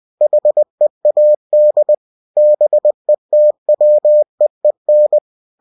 Morse code, heard as a keyed tone, about 20 wpm, 600 Hz. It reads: HEAD BETWEEN
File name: cw-books / winnie-the-pooh / winnie-20wpm-page-0005